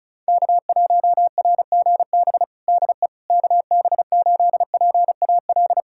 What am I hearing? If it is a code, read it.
Morse code, heard as a keyed tone, about 36 wpm, 700 Hz. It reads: K1RGB DE KB8PAL
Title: USA Callsign DE Callsign 35wpm